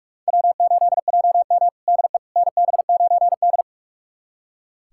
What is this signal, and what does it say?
Morse code, heard as a keyed tone, about 44 wpm, 700 Hz. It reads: W8JM DE NB9D
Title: USA Callsign DE Callsign - 4 Characters - 45wpm